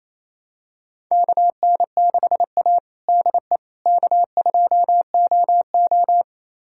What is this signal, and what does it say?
Morse code, heard as a keyed tone, about 28 wpm, 700 Hz. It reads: KN6A DE K2OO
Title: USA Callsign DE Callsign - 4 Characters - 28wpm